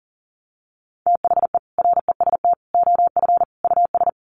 Morse code, heard as a keyed tone, about 37 wpm, 700 Hz. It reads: THE REST OF US
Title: Sets of 4 Words 40wpm